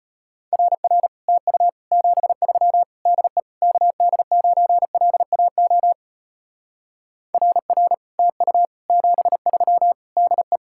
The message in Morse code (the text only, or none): RR TU 73 DE KD9LAO RR TU 73 DE